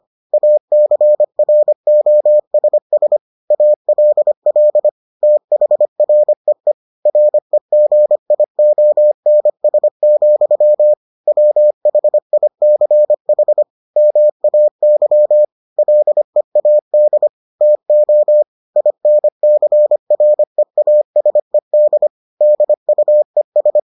Morse code, heard as a keyed tone, about 25 wpm, 600 Hz. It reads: ACROSS ALL THREE REGIONS, WHICH MAY LEAD TO INCREASED DUES